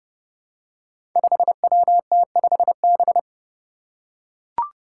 Morse code, heard as a keyed tone, about 30 wpm, 700 Hz. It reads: 5WT5B
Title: Five Challenging Letter-Number Combinations 30wpm